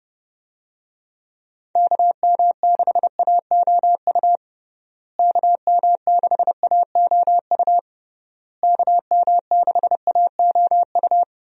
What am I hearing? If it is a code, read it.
Morse code, heard as a keyed tone, about 30 wpm, 700 Hz. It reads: KM6AOU KM6AOU KM6AOU
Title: USA Call Signs - 6 Characters - Repeated 3x - 30wpm